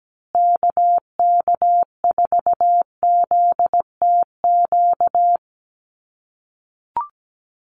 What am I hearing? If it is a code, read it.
Morse code, heard as a keyed tone, about 17 wpm, 700 Hz. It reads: KK4ZTQ
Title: USA Call Signs Repeated 3x 17wpm